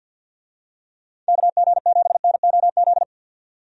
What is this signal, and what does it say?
Morse code, heard as a keyed tone, about 51 wpm, 700 Hz. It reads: KG7NOZ
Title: USA Call Signs - 6 Characters - 50wpm